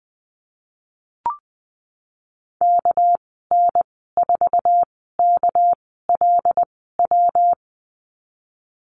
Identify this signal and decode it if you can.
Morse code, heard as a keyed tone, about 19 wpm, spaced as for 14 wpm, 700 Hz. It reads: KN4KLW
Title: 219 Lesson 35g - Number 0 - Call Signs 20wpm